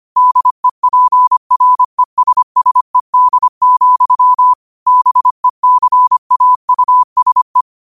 Morse code, heard as a keyed tone, about 25 wpm, 1000 Hz. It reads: DEPRESSED, BECAUSE
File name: qotdfiles / 25wpm